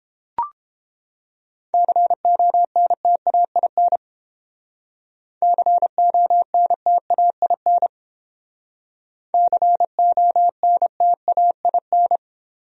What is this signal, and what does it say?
Morse code, heard as a keyed tone, about 31 wpm, 700 Hz. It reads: CONTAIN CONTAIN CONTAIN